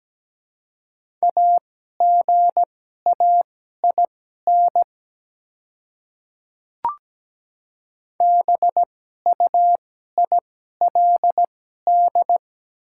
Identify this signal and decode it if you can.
Morse code, heard as a keyed tone, about 17 wpm, spaced as for 12 wpm, 700 Hz. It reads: AGAIN BUILD
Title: 2x Character Spacing - Top 200 Words - 17wpm